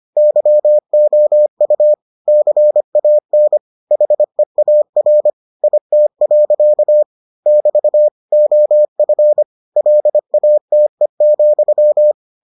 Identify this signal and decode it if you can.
Morse code, heard as a keyed tone, about 25 wpm, 600 Hz. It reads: YOU CAN HEAR IT. <BT> OF LATE,